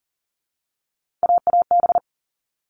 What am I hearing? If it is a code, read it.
Morse code, heard as a keyed tone, about 40 wpm, 700 Hz. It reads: AAB
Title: All Cut Numbers - 3-Digit 40wpm